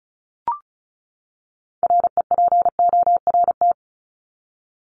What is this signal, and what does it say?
Morse code, heard as a keyed tone, about 34 wpm, 700 Hz. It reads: REPORT
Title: ED Suffix - Top 100 Words with Base Words 35wpm